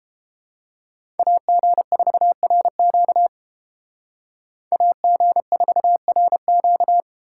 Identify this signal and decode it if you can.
Morse code, heard as a keyed tone, about 29 wpm, 700 Hz. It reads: AG4RQ AG4RQ